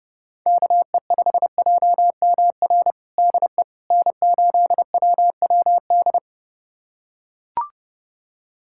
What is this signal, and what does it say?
Morse code, heard as a keyed tone, about 30 wpm, 700 Hz. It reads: KE5JMR DE N8WWD
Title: USA Callsign DE Callsign 30wpm